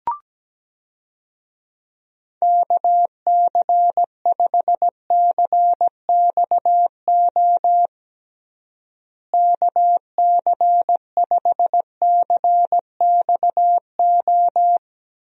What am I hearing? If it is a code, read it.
Morse code, heard as a keyed tone, about 17 wpm, 700 Hz. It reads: KC5CXO KC5CXO
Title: USA Call Signs - 6 Characters - Repeated 2x - 17wpm